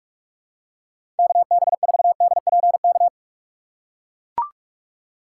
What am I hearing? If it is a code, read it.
Morse code, heard as a keyed tone, about 44 wpm, 700 Hz. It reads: KB4DPK
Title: USA Call Signs - 6 Characters - 6x Character Spacing - 45wpm